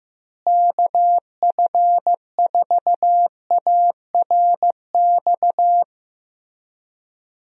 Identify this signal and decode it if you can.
Morse code, heard as a keyed tone, about 15 wpm, 700 Hz. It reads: KF4ARX
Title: USA Call Signs - 6 Characters - Repeated 2x - 15wpm